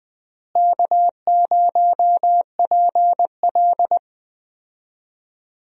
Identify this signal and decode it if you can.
Morse code, heard as a keyed tone, about 20 wpm, 700 Hz. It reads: K0PL
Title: USA Call Signs - 4 Characters - Repeated 3x - 20wpm